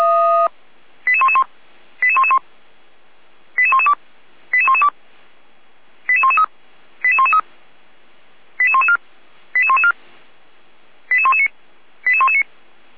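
Five-tone selective call (ZVEI), heard as quick runs of five tones, tones of about 70 ms. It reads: 88111 88111 88112 88112 88113 88113 88114 88114 88119 88119
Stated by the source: ZVEI five-tone selective call received on an SDR, from the feuerwehr-orlamuende fire brigade audio archive.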